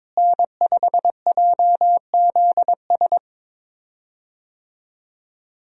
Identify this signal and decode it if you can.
Morse code, heard as a keyed tone, about 22 wpm, 700 Hz. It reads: N5JZS
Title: USA Call Signs - 5 Characters - Repeated 2x - 22wpm